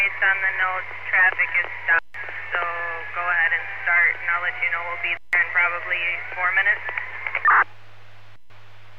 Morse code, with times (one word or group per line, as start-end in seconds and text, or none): none